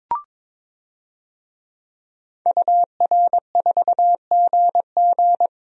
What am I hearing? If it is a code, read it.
Morse code, heard as a keyed tone, about 22 wpm, 700 Hz. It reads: UR4GG
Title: DXCC-100 Call Signs - 5 Characters - 22wpm